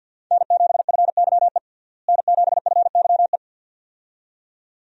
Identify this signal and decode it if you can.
Morse code, heard as a keyed tone, about 51 wpm, 700 Hz. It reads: N7FYE N7FYE
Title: USA Call Signs - 5 Characters - Repeated 2x - 50wpm